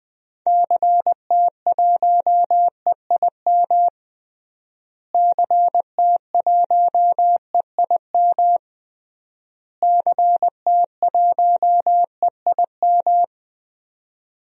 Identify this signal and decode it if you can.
Morse code, heard as a keyed tone, about 20 wpm, 700 Hz. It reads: CT1EIM CT1EIM CT1EIM
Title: DXCC-100 Call Signs - 6 Characters - Repeated 3x - 20wpm